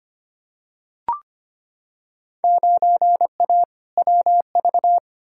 Morse code, heard as a keyed tone, about 25 wpm, 700 Hz. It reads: 9A WV
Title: Field Day Exchanges - 25wpm